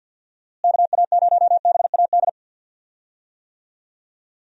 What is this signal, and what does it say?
Morse code, heard as a keyed tone, about 50 wpm, 700 Hz. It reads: KA0BAD
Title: USA Call Signs - 6 Characters - Repeated 2x - 50wpm